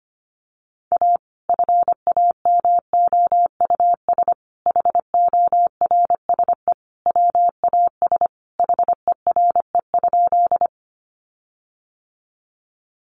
Morse code, heard as a keyed tone, about 25 wpm, 700 Hz. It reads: A FAMOUS HORSE WAS HERE?
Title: Sentences from Top 1500 Words 25wpm